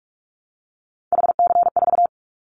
Morse code, heard as a keyed tone, about 46 wpm, 700 Hz. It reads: HC4
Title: Callsign Prefixes for Top 100 DXCC - 45wpm